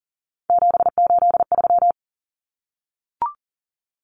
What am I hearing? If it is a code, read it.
Morse code, heard as a keyed tone, about 37 wpm, 700 Hz. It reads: 783
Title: Common Cut Numbers - 3-Digit 40wpm